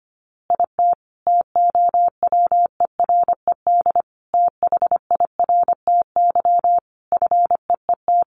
Morse code, heard as a keyed tone, about 25 wpm, 700 Hz. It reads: IT TOWERED THIRTY FEET
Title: A Sound Of Thunder 25wpm